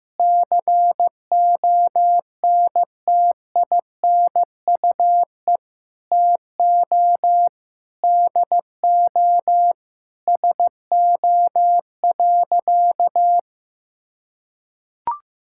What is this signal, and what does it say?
Morse code, heard as a keyed tone, about 16 wpm, 700 Hz. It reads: CONTINUE TO DO SO.
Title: Sentences from Top 1000 Words - Review Part 6 - 15wpm